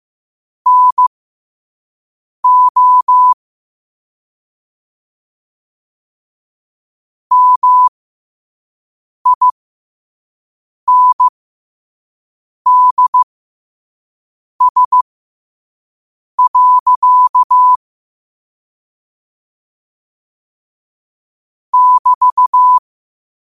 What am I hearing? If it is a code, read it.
Morse code, heard as a keyed tone, about 15 wpm, spaced as for 5 wpm, 1000 Hz. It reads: NO MINDS. <BT>
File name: 5wpm